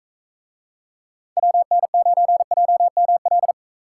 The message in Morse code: WN9JML

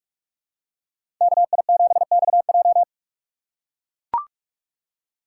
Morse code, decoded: KI7XJ

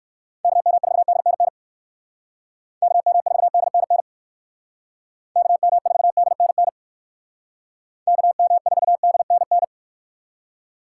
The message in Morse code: KM4DNN KM4DNN KM4DNN KM4DNN

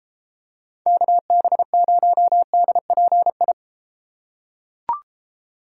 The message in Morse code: KB0DPI

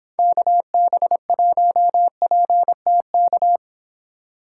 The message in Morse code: KB1PTK

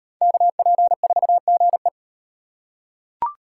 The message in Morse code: KP4GE

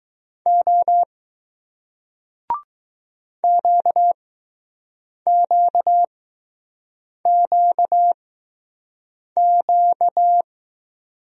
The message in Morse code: O QQQQ

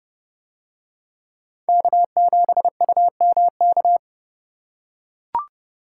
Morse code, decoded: K7UMK

